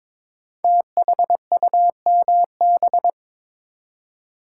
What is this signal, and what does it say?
Morse code, heard as a keyed tone, about 23 wpm, 700 Hz. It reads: THUMB
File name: 5 Letter Words - First 3 Letters Given - 22wpm